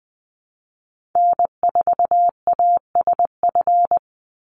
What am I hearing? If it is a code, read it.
Morse code, heard as a keyed tone, about 20 wpm, 700 Hz. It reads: N4ASF
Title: Call Signs - 20wpm